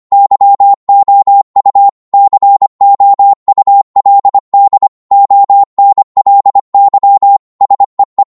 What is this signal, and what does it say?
Morse code, heard as a keyed tone, about 25 wpm, 800 Hz. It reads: YOU COULD ONLY SEE